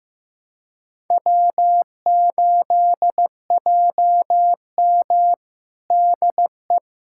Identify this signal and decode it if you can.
Morse code, heard as a keyed tone, about 15 wpm, 700 Hz. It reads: W8JM DE
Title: USA Callsign DE Callsign - 4 Characters - 15wpm